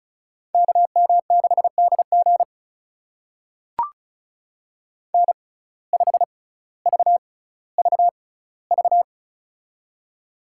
Morse code, heard as a keyed tone, about 36 wpm, spaced as for 18 wpm, 700 Hz. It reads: KM6DG N5VVV
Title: USA Call Signs - 5 Characters - 6x Character Spacing - 35wpm